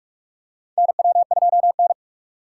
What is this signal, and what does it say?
Morse code, heard as a keyed tone, about 47 wpm, 700 Hz. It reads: NW2N
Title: USA Call Signs - 4 Characters - 3x Character Spacing - 45wpm